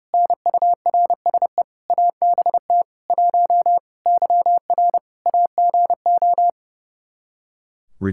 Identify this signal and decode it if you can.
Morse code, heard as a keyed tone, about 32 wpm, 700 Hz. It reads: NURSE ABT 1 YR AGO
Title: Practice QSO - Exchange 3 - More Info - 30wpm